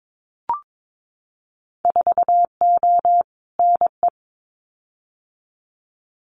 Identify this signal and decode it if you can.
Morse code, heard as a keyed tone, about 22 wpm, 700 Hz. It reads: 4O NE
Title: Winter Field Day Exchanges Encore 22wpm